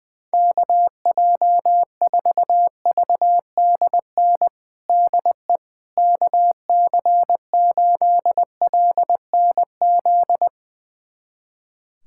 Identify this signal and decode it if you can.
Morse code, heard as a keyed tone, about 20 wpm, 700 Hz. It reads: KJ4VDN DE KC8LNZ
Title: USA Callsign DE Callsign - 6 Characters - 20wpm